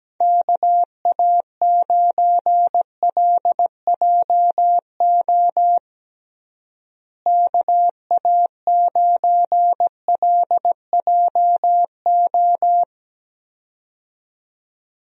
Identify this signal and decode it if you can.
Morse code, heard as a keyed tone, about 17 wpm, 700 Hz. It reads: KA9LJO KA9LJO
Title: USA Call Signs - 6 Characters - Repeated 2x - 17wpm